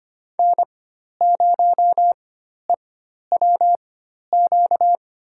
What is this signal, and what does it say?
Morse code, heard as a keyed tone, about 25 wpm, spaced as for 12 wpm, 700 Hz. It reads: N0EWQ
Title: USA Call Signs - 4x Character Spacing - 25wpm